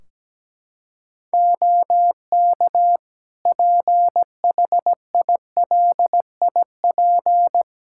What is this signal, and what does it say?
Morse code, heard as a keyed tone, about 17 wpm, 700 Hz. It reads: OK PHILIP
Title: Practice QSO - Exchange 4 - Ending - 17wpm